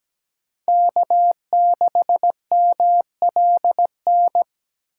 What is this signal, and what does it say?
Morse code, heard as a keyed tone, about 17 wpm, 700 Hz. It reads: K6MLN
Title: USA Call Signs - 5 Characters - 8x Character Spacing - 17wpm